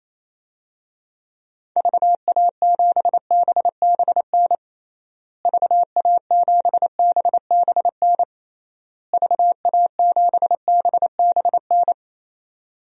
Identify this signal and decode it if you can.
Morse code, heard as a keyed tone, about 27 wpm, 700 Hz. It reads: VA7BBN VA7BBN VA7BBN
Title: DXCC-100 Call Signs - 6 Characters - Repeated 3x - 28wpm